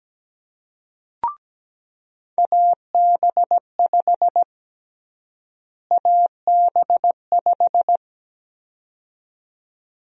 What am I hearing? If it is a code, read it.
Morse code, heard as a keyed tone, about 17 wpm, 700 Hz. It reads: AB5 AB5
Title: USA Call Sign Prefixes - Repeated 2x - 17wpm